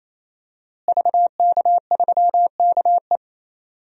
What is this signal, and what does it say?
Morse code, heard as a keyed tone, about 27 wpm, 700 Hz. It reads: VK3KE
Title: DXCC-100 Call Signs - 5 Characters - Repeated 3x - 28wpm